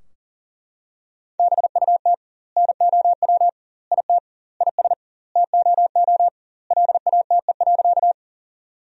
Morse code, heard as a keyed tone, about 40 wpm, 700 Hz. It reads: BUT NOW IT IS TOO LATE.
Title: 2x Word-Spacing - Sentences from Top 300 Words - 40wpm